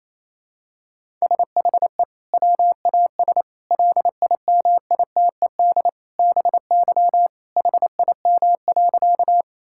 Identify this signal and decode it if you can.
Morse code, heard as a keyed tone, about 28 wpm, 700 Hz. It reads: SHE WAS LIMITED BY HIM.
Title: Sentences from Top 1000 Words - Review Part 6 - 28wpm